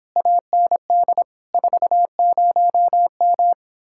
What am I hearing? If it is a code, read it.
Morse code, heard as a keyed tone, about 29 wpm, 700 Hz. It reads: AND 40M